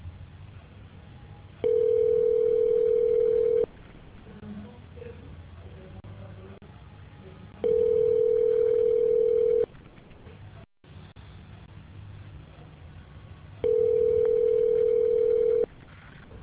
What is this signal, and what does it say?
Ringback tone (North America), 440 + 480 Hz.